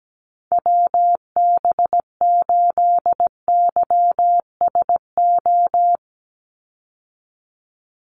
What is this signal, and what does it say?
Morse code, heard as a keyed tone, about 17 wpm, 700 Hz. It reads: WB8YSO
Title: USA Call Signs 17wpm